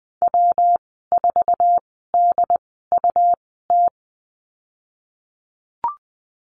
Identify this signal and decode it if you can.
Morse code, heard as a keyed tone, about 20 wpm, spaced as for 14 wpm, 700 Hz. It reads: W4DUT
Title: 140 Lesson 23g - Letter f - Call Signs 20wpm